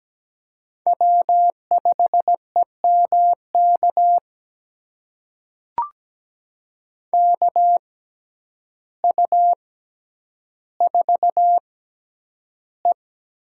Morse code, heard as a keyed tone, about 17 wpm, spaced as for 9 wpm, 700 Hz. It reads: W5EMK KU4E
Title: USA Call Signs - 5 Characters - 6x Character Spacing - 17wpm